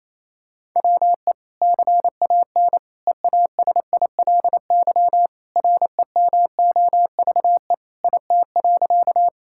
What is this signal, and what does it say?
Morse code, heard as a keyed tone, about 28 wpm, 700 Hz. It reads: WE CAN EASILY REMOVE IT.
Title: Sentences from Top 1000 Words - Review Part 5 - 28wpm